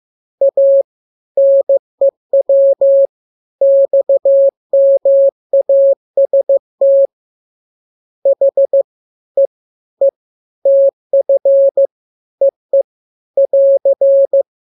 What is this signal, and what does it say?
Morse code, heard as a keyed tone, about 15 wpm, spaced as for 12 wpm, 550 Hz. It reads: A NEW XMAS THE E TF EE <AR>